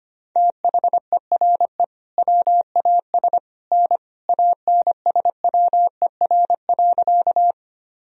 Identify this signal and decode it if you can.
Morse code, heard as a keyed tone, about 25 wpm, 700 Hz. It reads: THERE WAS N ANSWER.
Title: Sentences from Top 300 Words and Abbreviations 25wpm